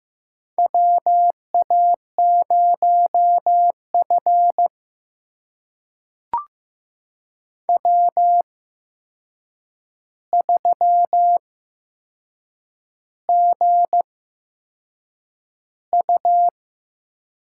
Morse code, heard as a keyed tone, about 15 wpm, spaced as for 6 wpm, 700 Hz. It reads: WA0F W3GU
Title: USA Call Signs - 4 Characters - 8x Character Spacing - 15wpm